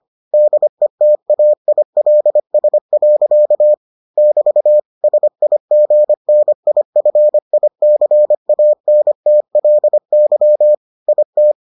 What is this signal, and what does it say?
Morse code, heard as a keyed tone, about 25 wpm, 600 Hz. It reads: DETAILS. <BT> SIGNIFICANTLY IT